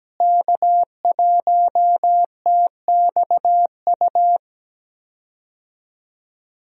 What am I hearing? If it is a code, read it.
Morse code, heard as a keyed tone, about 17 wpm, 700 Hz. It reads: K1TXU
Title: USA Call Signs - 5 Characters - Repeated 2x - 17wpm